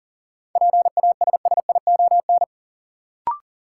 Morse code, heard as a keyed tone, about 40 wpm, 700 Hz. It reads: PASSION
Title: Sentences from Top 1300 Words - New Word Drill Alt - 40wpm